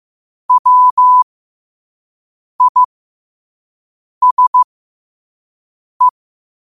Morse code, heard as a keyed tone, about 15 wpm, spaced as for 5 wpm, 1000 Hz. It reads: WISE
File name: qotdfiles / 5wpm